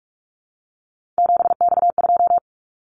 Morse code, heard as a keyed tone, about 46 wpm, 700 Hz. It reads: 7X2
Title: Callsign Prefixes for Top 100 DXCC - 45wpm